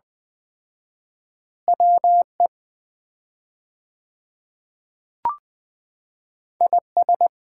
Morse code, heard as a keyed tone, about 20 wpm, 700 Hz. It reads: WE IS
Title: 4x Word-Spacing - Sets of 5 Words - 20wpm